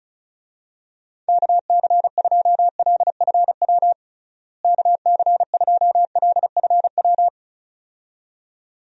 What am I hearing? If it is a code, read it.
Morse code, heard as a keyed tone, about 36 wpm, 700 Hz. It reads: KC2LFW KC2LFW
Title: USA Call Signs - 6 Characters - Repeated 2x - 35wpm